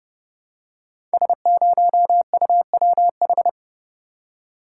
Morse code, heard as a keyed tone, about 30 wpm, 700 Hz. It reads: S0UWH
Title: Code Groups - 5 Letter:Numbers - 30wpm